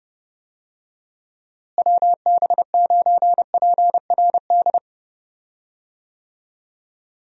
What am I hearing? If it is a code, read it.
Morse code, heard as a keyed tone, about 30 wpm, 700 Hz. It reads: WB9PRD
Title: USA Call Signs - 6 Characters - Repeated 2x - 30wpm